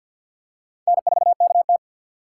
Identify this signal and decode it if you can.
Morse code, heard as a keyed tone, about 48 wpm, 700 Hz. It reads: N4KT
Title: USA Call Signs - 3x Character Spacing - 50wpm